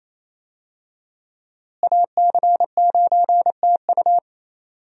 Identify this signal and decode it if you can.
Morse code, heard as a keyed tone, about 28 wpm, 700 Hz. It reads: AC9TU
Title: USA Call Signs - 5 Characters - Repeated 2x - 28wpm